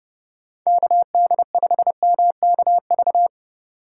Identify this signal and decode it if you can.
Morse code, heard as a keyed tone, about 30 wpm, 700 Hz. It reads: KD5MKV
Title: USA Call Signs - 6 Characters - 3x Character Spacing - 30wpm